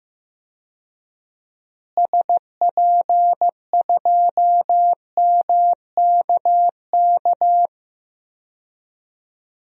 Morse code, heard as a keyed tone, about 15 wpm, 700 Hz. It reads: SP2MKK
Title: DXCC-100 Call Signs - 6 Characters - Repeated 3x - 15wpm